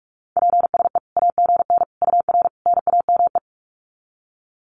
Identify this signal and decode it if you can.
Morse code, heard as a keyed tone, about 44 wpm, 700 Hz. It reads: PSE AGN UR NAME
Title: Common QSO Phrases 45wpm